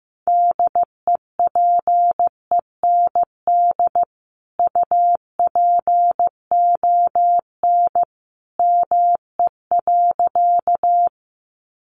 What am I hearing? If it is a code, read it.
Morse code, heard as a keyed tone, about 15 wpm, 700 Hz. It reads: DEPEND UPON ME.
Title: Sentences from Top 1300 Words 15wpm